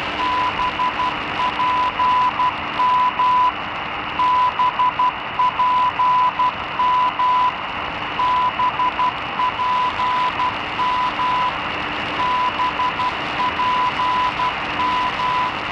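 Morse code, heard as a keyed tone, about 12 wpm, 1000 Hz, with noise. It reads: BPM BPM BPM BPM
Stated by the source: BPM time signal station of the Chinese Academy of Sciences National Time Service Center on 15 MHz, received at OM89dx with SDR RSP1 and active small loop antenna, good propagation, slight fading at end.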